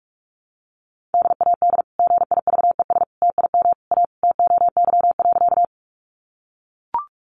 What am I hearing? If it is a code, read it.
Morse code, heard as a keyed tone, about 44 wpm, 700 Hz. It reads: DAD GIVES TIM A TOY.
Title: Easy Phrases 45wpm